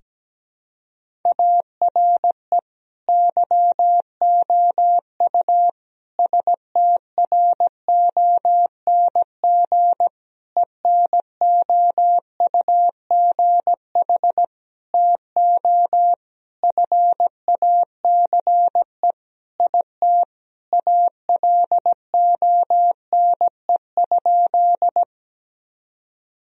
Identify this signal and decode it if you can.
Morse code, heard as a keyed tone, about 17 wpm, 700 Hz. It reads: ARE YOU STRONG ENOUGH TO FACE IT ALONE?
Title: Sentences from Top 1000 Words - Review Part 3 - 17wpm